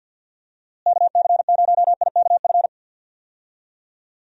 Morse code, heard as a keyed tone, about 52 wpm, 700 Hz. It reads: KC0IKF
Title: USA Call Signs - 6 Characters - Repeated 3x - 50wpm